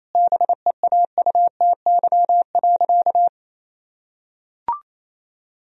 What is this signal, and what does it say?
Morse code, heard as a keyed tone, about 28 wpm, 700 Hz. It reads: BEAUTY.
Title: Sentences from Top 1000 Words - Review Part 2 - 28wpm